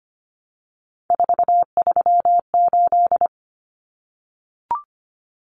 Morse code, heard as a keyed tone, about 25 wpm, 700 Hz. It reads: 438
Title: Common Cut Numbers - 3-Digit 25wpm